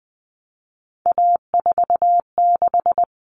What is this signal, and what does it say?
Morse code, heard as a keyed tone, about 20 wpm, 700 Hz. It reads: A46
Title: Common Cut Numbers - 3-Digit 20wpm